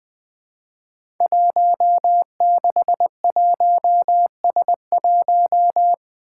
Morse code, heard as a keyed tone, about 20 wpm, 700 Hz. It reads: 161S1
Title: Five Challenging Letter-Number Combinations 20wpm